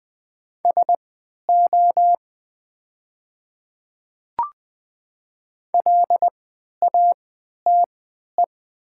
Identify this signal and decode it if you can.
Morse code, heard as a keyed tone, about 20 wpm, spaced as for 11 wpm, 700 Hz. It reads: SO LATE
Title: 3x Character Spacing - Top 300 Words - 20wpm